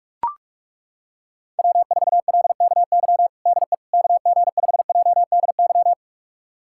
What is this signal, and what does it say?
Morse code, heard as a keyed tone, about 43 wpm, 700 Hz. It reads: W4LKY DE KG5JDY
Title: USA Callsign DE Callsign 45wpm